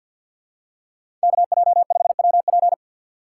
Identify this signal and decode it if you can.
Morse code, heard as a keyed tone, about 54 wpm, 700 Hz. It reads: KJ5WP
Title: USA Call Signs - 6x Character Spacing - 50wpm